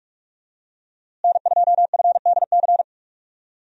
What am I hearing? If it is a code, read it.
Morse code, heard as a keyed tone, about 46 wpm, 700 Hz. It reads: N2FDC
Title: USA Call Signs - 5 Characters - Repeated 2x - 45wpm